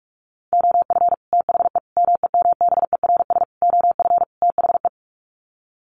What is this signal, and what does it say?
Morse code, heard as a keyed tone, about 44 wpm, 700 Hz. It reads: OF THE MEMBERS OF THE
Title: Sets of 5 Words Encore2 45wpm